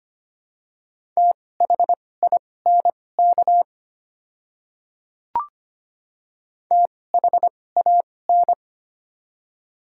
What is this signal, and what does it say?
Morse code, heard as a keyed tone, about 25 wpm, spaced as for 18 wpm, 700 Hz. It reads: THINK THAN
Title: 2x Character Spacing - Top 200 Words - 25wpm